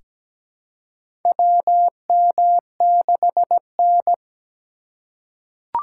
Morse code, heard as a keyed tone, about 17 wpm, 700 Hz. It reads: WM6N
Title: USA Call Signs - 4 Characters - 3x Character Spacing - 17wpm